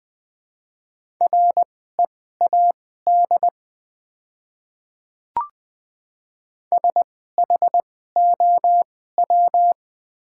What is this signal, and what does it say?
Morse code, heard as a keyed tone, about 21 wpm, spaced as for 15 wpm, 700 Hz. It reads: READ SHOW
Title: 2x Character Spacing - Top 200 Words - 20wpm